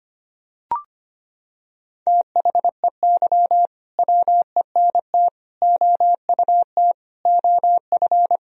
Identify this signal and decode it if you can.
Morse code, heard as a keyed tone, about 25 wpm, 700 Hz. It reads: THEY WENT OUT OF
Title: Sentences from Top 1000 Words - Review Part 4 - 25wpm